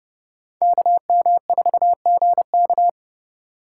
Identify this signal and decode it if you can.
Morse code, heard as a keyed tone, about 36 wpm, 700 Hz. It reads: KM4GK
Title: Speed Racing - USA Call Signs 30wpm